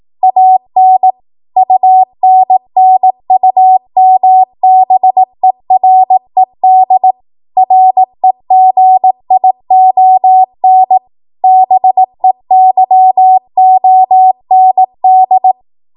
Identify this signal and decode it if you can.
Morse code, heard as a keyed tone, about 18 wpm, 750 Hz. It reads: AN UNNUMBERED REGION BEYOND
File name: Morse / 18WPM